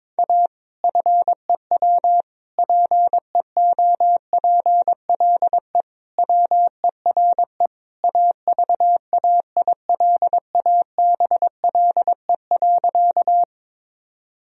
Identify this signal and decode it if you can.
Morse code, heard as a keyed tone, about 22 wpm, 700 Hz. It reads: A FEW PEOPLE WERE AVAILABLE.
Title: Sentences from Top 1000 Words - Review Part 3 - 22wpm